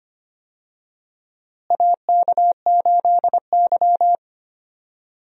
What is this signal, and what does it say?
Morse code, heard as a keyed tone, about 25 wpm, 700 Hz. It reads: AK8Y
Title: USA Call Signs - 4 Characters - Repeated 2x - 25wpm